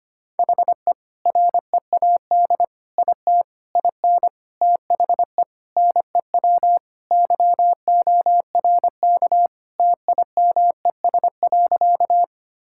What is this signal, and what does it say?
Morse code, heard as a keyed tone, about 25 wpm, 700 Hz. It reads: HE READ IT IN THE NEW YORK TIMES.
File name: practice / 25wpm / Sentences with 4-Word Phrases - 25wpm